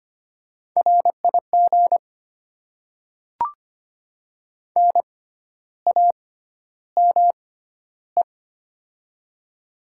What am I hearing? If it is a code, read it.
Morse code, heard as a keyed tone, about 25 wpm, spaced as for 12 wpm, 700 Hz. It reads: RIG NAME
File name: Top 20 QSO Elements - 6x Character Spacing - 25wpm